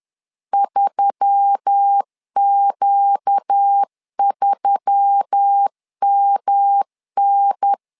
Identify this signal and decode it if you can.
Morse code, heard as a keyed tone, about 10 wpm, 800 Hz. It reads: 3Q3MN
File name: LEZIONI / 4 Q1S3 10wpm 2427